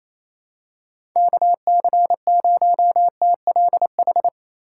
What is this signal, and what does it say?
Morse code, heard as a keyed tone, about 28 wpm, 700 Hz. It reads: KC0TLH
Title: USA Call Signs - 6 Characters - 3x Character Spacing - 28wpm